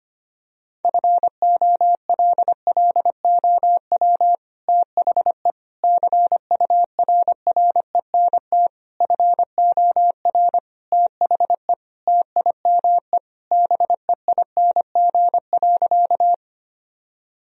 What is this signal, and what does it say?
Morse code, heard as a keyed tone, about 25 wpm, 700 Hz. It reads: FOLLOW THE CURRENT FOR THE TIME BEING.
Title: Sentences from Top 1000 Words - Review Part 4 - 25wpm